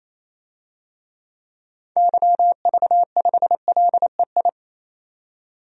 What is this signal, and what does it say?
Morse code, heard as a keyed tone, about 28 wpm, 700 Hz. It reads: YV5LEI
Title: DXCC-100 Call Signs - 6 Characters - 28wpm